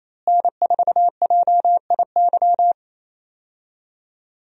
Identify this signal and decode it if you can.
Morse code, heard as a keyed tone, about 28 wpm, 700 Hz. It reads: N4JIY
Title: USA Call Signs - 5 Characters - Repeated 3x - 28wpm